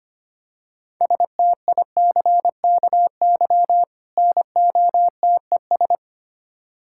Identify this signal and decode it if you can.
Morse code, heard as a keyed tone, about 25 wpm, 700 Hz. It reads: STICKY NOTES
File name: practice / 25wpm / Difficult Words with Hints - 25wpm